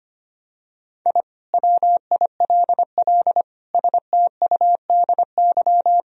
Morse code, heard as a keyed tone, about 25 wpm, 700 Hz. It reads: I WILL STUDY
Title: Sentences from Top 1000 Words - Review Part 2 - 25wpm